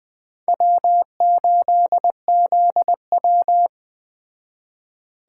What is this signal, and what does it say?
Morse code, heard as a keyed tone, about 21 wpm, 700 Hz. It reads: W8ZW